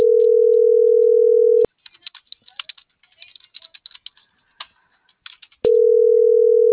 Ringback tone (North America), 440 + 480 Hz.